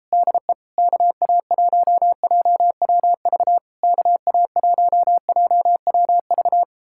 Morse code, heard as a keyed tone, about 38 wpm, 700 Hz. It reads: DE KA1JWV KA1JWV